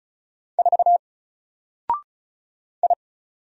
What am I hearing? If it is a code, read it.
Morse code, heard as a keyed tone, about 35 wpm, 700 Hz. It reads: 4 I
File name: Single-Letter-Number-35wpm